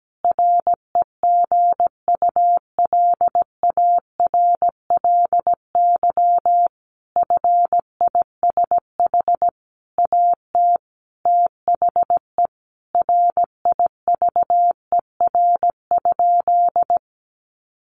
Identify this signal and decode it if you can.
Morse code, heard as a keyed tone, about 17 wpm, 700 Hz. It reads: REGULARLY FISH AT THE RIVER?
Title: Sentences from Top 1400 Words 17wpm